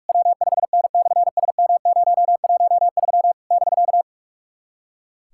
Morse code, heard as a keyed tone, about 46 wpm, 700 Hz. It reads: W5N/SM013 <BK>
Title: SOTA QSO Practice 45wpm